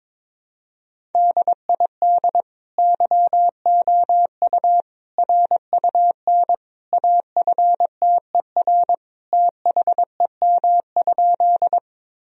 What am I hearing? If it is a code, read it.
Morse code, heard as a keyed tone, about 22 wpm, 700 Hz. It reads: DID YOU RUN AFTER THEM?
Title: Sentences from Top 1000 Words - Review Part 1 - 22wpm